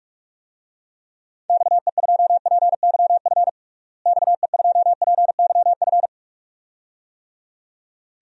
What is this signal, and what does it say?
Morse code, heard as a keyed tone, about 46 wpm, 700 Hz. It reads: XE2PYF XE2PYF
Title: DXCC-100 Call Signs - 6 Characters - Repeated 2x - 45wpm